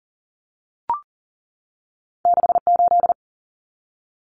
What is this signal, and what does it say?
Morse code, heard as a keyed tone, about 40 wpm, 700 Hz. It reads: 68
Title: Common Cut Numbers - 2-Digit 40wpm